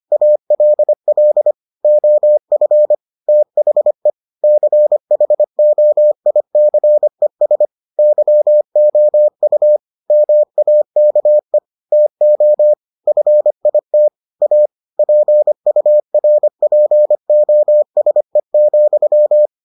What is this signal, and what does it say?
Morse code, heard as a keyed tone, about 25 wpm, 600 Hz. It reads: ALL OF THE CHOICES YOU MAKE TO FIT A PURPOSE,